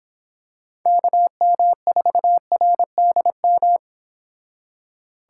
Morse code, heard as a keyed tone, about 25 wpm, 700 Hz. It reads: KM4RDM